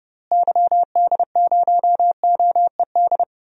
Morse code, heard as a keyed tone, about 30 wpm, 700 Hz. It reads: YD0OED
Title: DXCC-100 Call Signs - 6 Characters - Repeated 3x - 30wpm